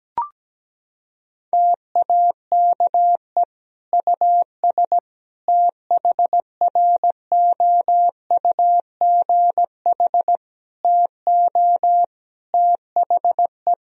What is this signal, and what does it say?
Morse code, heard as a keyed tone, about 17 wpm, 700 Hz. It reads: TAKE US THROUGH TO THE
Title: Sentences from Top 1000 Words - Review Part 1 - 17wpm